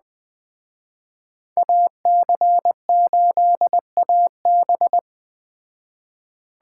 Morse code, heard as a keyed tone, about 20 wpm, 700 Hz. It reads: AC8AB